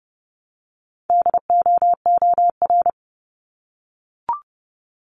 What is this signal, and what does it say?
Morse code, heard as a keyed tone, about 32 wpm, 700 Hz. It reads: DOOR
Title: Multi-letter Phonemes Repeated 3x 30wpm